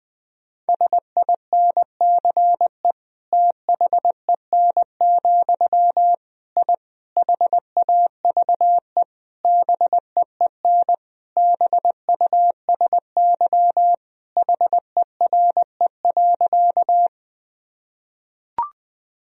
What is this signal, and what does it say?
Morse code, heard as a keyed tone, about 21 wpm, 700 Hz. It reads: SINCE THEN, I HAVE BEEN BUSY HERE.
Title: Sentences from Top 1000 Words - Review Part 3 - 20wpm